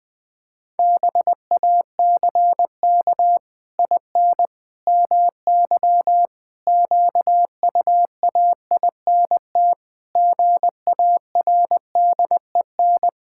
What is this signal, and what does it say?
Morse code, heard as a keyed tone, about 20 wpm, 700 Hz. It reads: BACK IN MY QUAINT GARDEN
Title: Pangram Challenge v2 - 20wpm